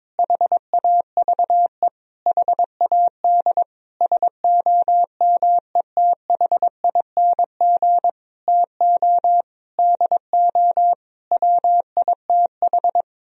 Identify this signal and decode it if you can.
Morse code, heard as a keyed tone, about 22 wpm, 700 Hz. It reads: HAVE HAD SOMETHING TO DO WITH
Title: Sentences from Top 1000 Words - Review Part 2 - 22wpm